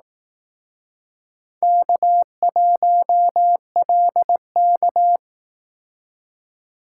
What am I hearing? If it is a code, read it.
Morse code, heard as a keyed tone, about 18 wpm, 700 Hz. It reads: K1LK
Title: Speed Racing - USA 1x2 Call Signs 15wpm